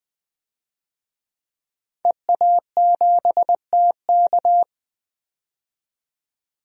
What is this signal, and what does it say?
Morse code, heard as a keyed tone, about 20 wpm, 700 Hz. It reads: EA7TK
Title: DXCC-100 Call Signs - 5 Characters - 20wpm